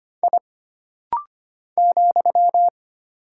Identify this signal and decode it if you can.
Morse code, heard as a keyed tone, about 25 wpm, 700 Hz. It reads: I ,
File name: Single Character (Letter:Number:Punctuation:Prosign) - Rapid-Fire - 25wpm